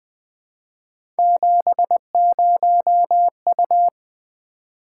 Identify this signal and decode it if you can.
Morse code, heard as a keyed tone, about 20 wpm, 700 Hz. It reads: 70U
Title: Three Letter-Number Combinations - 20wpm